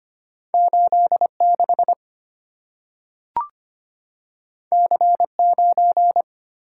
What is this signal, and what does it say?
Morse code, heard as a keyed tone, about 25 wpm, 700 Hz. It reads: 86 C9
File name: Two Letter-Number Combinations - 25wpm